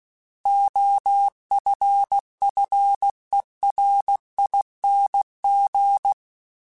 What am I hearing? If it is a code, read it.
Morse code, heard as a keyed tone, about 16 wpm, 800 Hz. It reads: OFFERING
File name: Special INGs / Special INGs-mp3s / Sngl ing @15